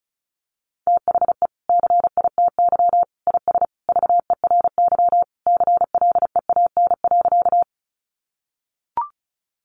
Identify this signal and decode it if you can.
Morse code, heard as a keyed tone, about 34 wpm, 700 Hz. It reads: THE CITY IS VERY CLEAN.
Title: Easy Phrases 35wpm